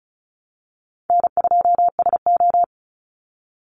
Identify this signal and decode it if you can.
Morse code, heard as a keyed tone, about 34 wpm, 700 Hz. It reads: N2SO
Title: Easy Call Signs Repeated 3x - 35wpm